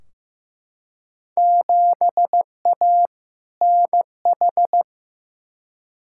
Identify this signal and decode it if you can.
Morse code, heard as a keyed tone, about 15 wpm, 700 Hz. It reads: 7A NH
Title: Field Day Exchanges - 15wpm